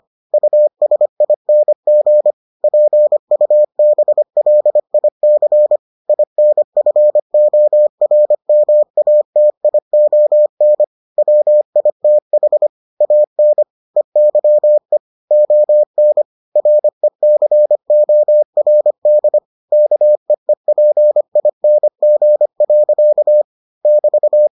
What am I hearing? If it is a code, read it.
Morse code, heard as a keyed tone, about 25 wpm, 600 Hz. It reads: USING PUBLIC INFORMATION WITH AN EYE ON RECORD KEEPING. <BT>